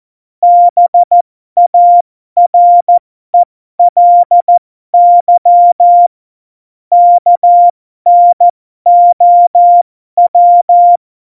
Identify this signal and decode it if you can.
Morse code, heard as a keyed tone, about 14 wpm, spaced as for 12 wpm, 700 Hz. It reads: BARELY KNOW
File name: Morsekurs / Witze